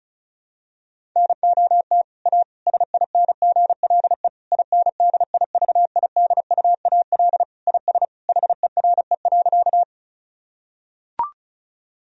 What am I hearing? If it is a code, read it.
Morse code, heard as a keyed tone, about 34 wpm, 700 Hz. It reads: NOT A SINGLE INDIVIDUAL IS HERE.
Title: Sentences from Top 1000 Words - Review Part 4 - 35wpm